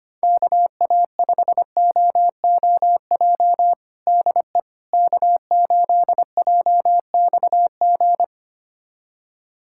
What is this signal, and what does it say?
Morse code, heard as a keyed tone, about 25 wpm, 700 Hz. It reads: KA5OOJ DE K8JXG
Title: USA Callsign DE Callsign 25wpm